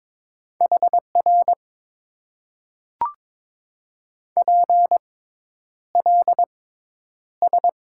Morse code, heard as a keyed tone, about 22 wpm, spaced as for 10 wpm, 700 Hz. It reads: HR PLS
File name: Top 90 QSO Elements - 6x Character Spacing - 22wpm